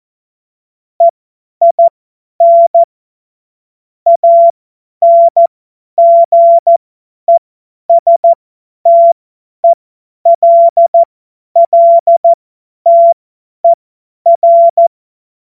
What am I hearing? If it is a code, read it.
Morse code, heard as a keyed tone, about 14 wpm, spaced as for 10 wpm, 680 Hz. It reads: EIN ANGESTELLTER
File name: Morsekurs / Witze / Witz 20. Dezember 2022 50 F70